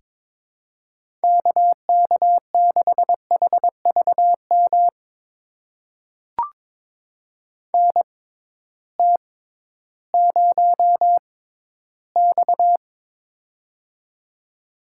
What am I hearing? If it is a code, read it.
Morse code, heard as a keyed tone, about 23 wpm, spaced as for 13 wpm, 700 Hz. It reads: KK6HVM NT0X
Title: USA Call Signs - 6x Character Spacing - 22wpm